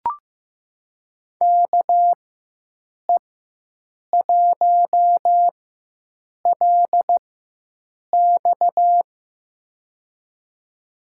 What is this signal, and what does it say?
Morse code, heard as a keyed tone, about 15 wpm, spaced as for 7 wpm, 700 Hz. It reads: KE1LX
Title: USA Call Signs - 5 Characters - 4x Character Spacing - 15wpm